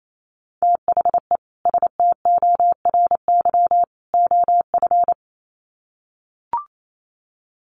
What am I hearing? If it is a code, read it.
Morse code, heard as a keyed tone, about 28 wpm, 700 Hz. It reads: THE STORY OF
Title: Sets of 3 Words 28wpm